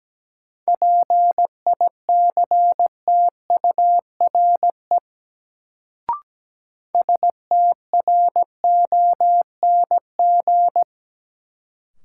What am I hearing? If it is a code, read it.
Morse code, heard as a keyed tone, about 17 wpm, 700 Hz. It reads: PICTURE STRONG
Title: Sentences from Top 500 Words - New Word Drill - 17wpm